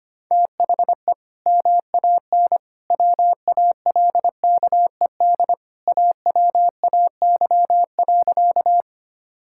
Morse code, heard as a keyed tone, about 25 wpm, 690 Hz. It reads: THE MAN WALKED AWAY.
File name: practice / 25wpm / Sentences with 2-Word Phrases - 25wpm